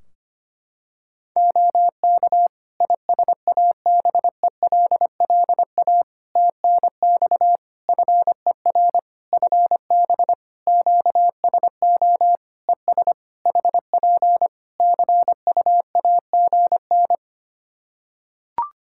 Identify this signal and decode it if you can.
Morse code, heard as a keyed tone, about 25 wpm, 700 Hz. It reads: OK ISABELLA TNX FER FB QSO ES HP CUAGN
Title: Practice QSO - Exchange 4 - Ending - 25wpm